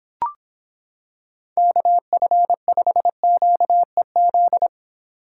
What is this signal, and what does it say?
Morse code, heard as a keyed tone, about 25 wpm, 700 Hz. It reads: KF5QEZ